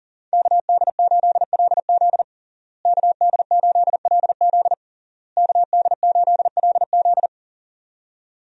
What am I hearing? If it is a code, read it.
Morse code, heard as a keyed tone, about 40 wpm, 700 Hz. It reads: KD8LZ KD8LZ KD8LZ
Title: USA Call Signs - 5 Characters - Repeated 3x - 40wpm